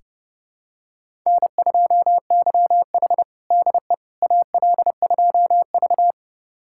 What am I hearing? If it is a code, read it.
Morse code, heard as a keyed tone, about 30 wpm, 700 Hz. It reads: N2YH DE AL2V
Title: USA Callsign DE Callsign - 4 Characters - 30wpm